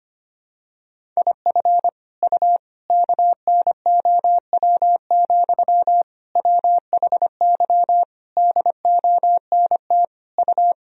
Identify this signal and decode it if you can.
Morse code, heard as a keyed tone, about 25 wpm, 700 Hz. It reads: IF U KNOW, WHY DONT U
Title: Sentences from Top 300 Words and Abbreviations 25wpm